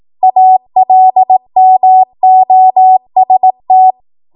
Morse code, heard as a keyed tone, about 18 wpm, 750 Hz. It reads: ALMOST
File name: Morse / 18WPM